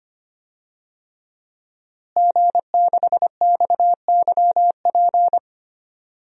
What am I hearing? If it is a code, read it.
Morse code, heard as a keyed tone, about 25 wpm, 700 Hz. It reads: G6XYP
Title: DXCC-100 Call Signs - 5 Characters - Repeated 3x - 25wpm